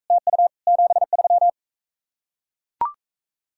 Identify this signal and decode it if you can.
Morse code, heard as a keyed tone, about 43 wpm, 700 Hz. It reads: TU 73